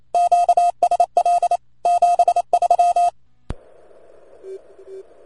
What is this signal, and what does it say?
Morse code, heard as a keyed tone, about 27 wpm, 700 Hz. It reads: QSL 73